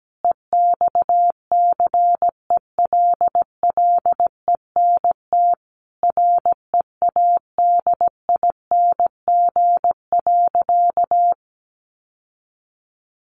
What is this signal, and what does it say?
Morse code, heard as a keyed tone, about 17 wpm, 700 Hz. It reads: EXCELLENT READING.
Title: Sentences from Top 1000 Words 17wpm